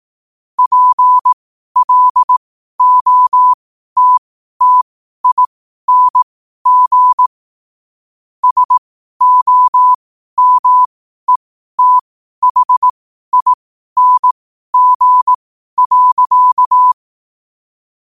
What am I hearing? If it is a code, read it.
Morse code, heard as a keyed tone, about 18 wpm, spaced as for 12 wpm, 1000 Hz. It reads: PLOTTING SOMETHING.
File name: qotdfiles / 12wpm